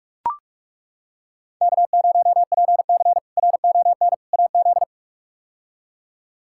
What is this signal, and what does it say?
Morse code, heard as a keyed tone, about 42 wpm, 700 Hz. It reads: K0PC RON AZ
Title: K1USN SST QSO Practice - 45wpm